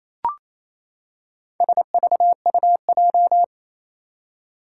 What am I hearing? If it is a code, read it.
Morse code, heard as a keyed tone, about 28 wpm, 700 Hz. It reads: SVUJ
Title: Four Challenging Letter-Number Combinations 28wpm